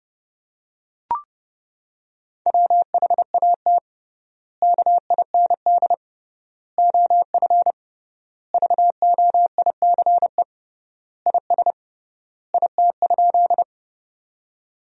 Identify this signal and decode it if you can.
Morse code, heard as a keyed tone, about 30 wpm, 700 Hz. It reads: WHAT KIND OF VOICE IS IT?
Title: 3x Word-Spacing - Sentences from Top 500 Words - 30wpm